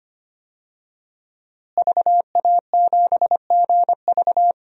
Morse code, heard as a keyed tone, about 25 wpm, 700 Hz. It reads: VA7GV
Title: DXCC-100 Call Signs - 5 Characters - Repeated 2x - 25wpm